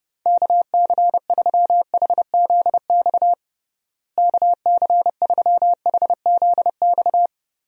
Morse code, heard as a keyed tone, about 30 wpm, 700 Hz. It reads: KC3HZX KC3HZX
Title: USA Call Signs - 6 Characters - Repeated 3x - 30wpm